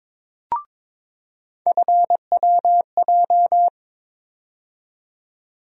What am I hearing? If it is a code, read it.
Morse code, heard as a keyed tone, about 21 wpm, 700 Hz. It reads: FWJ
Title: USA Call Sign Postfixes - 3 Letters 22wpm